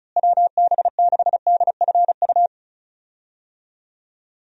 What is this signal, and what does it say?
Morse code, heard as a keyed tone, about 36 wpm, 700 Hz. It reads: WB6DFU
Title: USA Call Signs - 6 Characters - Repeated 2x - 35wpm